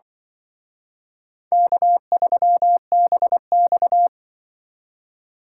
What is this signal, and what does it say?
Morse code, heard as a keyed tone, about 24 wpm, 700 Hz. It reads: K3BX
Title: Speed Racing - USA 1x2 Call Signs 20wpm